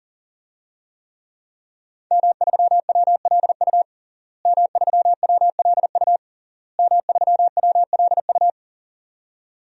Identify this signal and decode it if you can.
Morse code, heard as a keyed tone, about 40 wpm, 700 Hz. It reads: M3WLU M3WLU M3WLU
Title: DXCC-100 Call Signs - 5 Characters - Repeated 3x - 40wpm